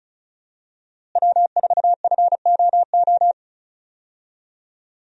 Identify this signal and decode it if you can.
Morse code, heard as a keyed tone, about 36 wpm, 700 Hz. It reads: W4FOO
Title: USA Call Signs - 5 Characters - Repeated 3x - 35wpm